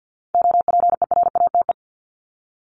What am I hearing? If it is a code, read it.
Morse code, heard as a keyed tone, about 48 wpm, 700 Hz. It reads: OPERATE
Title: ING Suffix - Top 100 Words with Base Words 50wpm